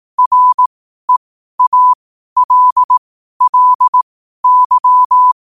Morse code, heard as a keyed tone, about 18 wpm, spaced as for 12 wpm, 1000 Hz. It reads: REALLY
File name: qotdfiles / 12wpm